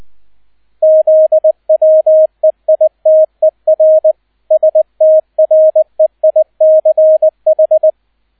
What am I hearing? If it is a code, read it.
Morse code, heard as a keyed tone, about 20 wpm, 630 Hz, with radio noise. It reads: ZWEITER STREICH
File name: 002 Zweiter Streich 20wpm